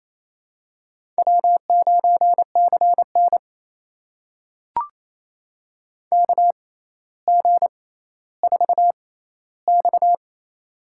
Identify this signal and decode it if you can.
Morse code, heard as a keyed tone, about 28 wpm, spaced as for 14 wpm, 700 Hz. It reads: W9CN KG4X
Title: USA Call Signs - 4 Characters - 6x Character Spacing - 28wpm